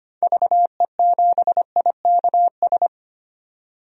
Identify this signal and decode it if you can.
Morse code, heard as a keyed tone, about 25 wpm, 700 Hz. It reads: VE7IKS
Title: DXCC-100 Call Signs - 6 Characters - 25wpm